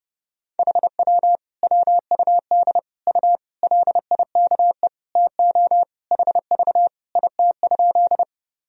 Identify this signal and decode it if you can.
Morse code, heard as a keyed tone, about 32 wpm, 700 Hz. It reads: HW WUD U LIKE TO HV IT?
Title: Sentences from Top 100 Words and Abbreviations 30wpm